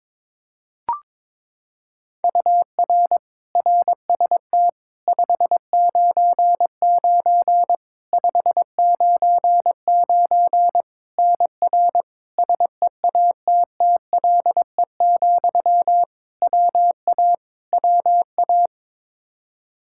Morse code, heard as a keyed tone, about 22 wpm, 700 Hz. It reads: UR RST 599 599 NR SEATTLE, WA WA
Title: Practice QSO - Exchange 1 - Beginning - 22wpm